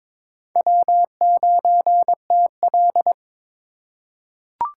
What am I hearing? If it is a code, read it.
Morse code, heard as a keyed tone, about 22 wpm, 700 Hz. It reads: W9TL
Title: USA Call Signs - 4 Characters - 3x Character Spacing - 22wpm